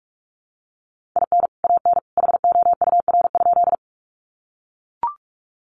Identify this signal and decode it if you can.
Morse code, heard as a keyed tone, about 44 wpm, 700 Hz. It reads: IN AN HOUR?
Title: Sentences from Top 400 Words 45wpm